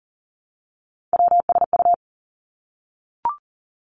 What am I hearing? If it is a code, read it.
Morse code, heard as a keyed tone, about 39 wpm, 700 Hz. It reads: WSU
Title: Three Challenging Letter-Number Combinations 40wpm